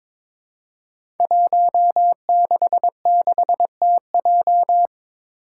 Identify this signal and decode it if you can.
Morse code, heard as a keyed tone, about 22 wpm, 700 Hz. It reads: 166TJ
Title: Five Challenging Letter-Number Combinations 22wpm